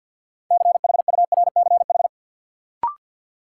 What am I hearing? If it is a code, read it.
Morse code, heard as a keyed tone, about 53 wpm, 700 Hz. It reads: CHURCH
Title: Sets of 5 Words - New Word Drill - 50wpm